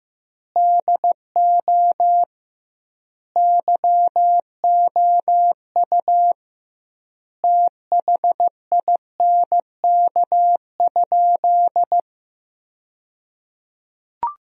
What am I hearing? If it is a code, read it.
Morse code, heard as a keyed tone, about 15 wpm, 700 Hz. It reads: DO YOU THINK?
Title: 2x Word-Spacing - Sentences from Top 200 Words - 15wpm